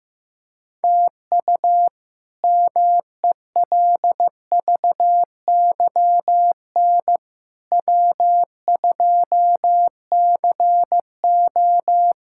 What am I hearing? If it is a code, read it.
Morse code, heard as a keyed tone, about 16 wpm, 700 Hz. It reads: TU MELVYN W2CO
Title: K1USN SST QSO Practice - 15wpm